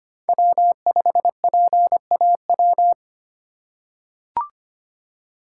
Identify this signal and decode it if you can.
Morse code, heard as a keyed tone, about 25 wpm, 700 Hz. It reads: W5PAW
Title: USA Call Signs - 5 Characters - 3x Character Spacing - 25wpm